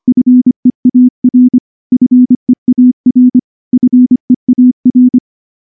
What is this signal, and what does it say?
Morse code, heard as a keyed tone, about 23 wpm, 250 Hz. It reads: FEAR FEAR FEAR
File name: morse-training-25WPM